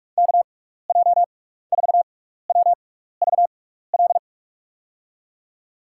Morse code, heard as a keyed tone, about 44 wpm, spaced as for 15 wpm, 700 Hz. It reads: KJ4WVL
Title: USA Call Signs - 6 Characters - 6x Character Spacing - 45wpm